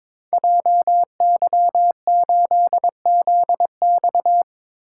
Morse code, heard as a keyed tone, about 22 wpm, 690 Hz. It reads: JY8ZX
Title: Code Groups - 5 Letter:Numbers - 22wpm